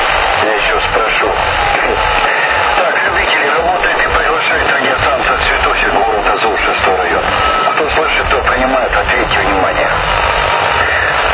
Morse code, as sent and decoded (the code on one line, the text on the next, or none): none